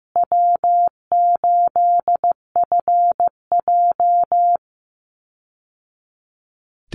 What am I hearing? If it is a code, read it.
Morse code, heard as a keyed tone, about 15 wpm, 700 Hz. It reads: W8FJ
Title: Easy Call Signs Repeated 2x - 15wpm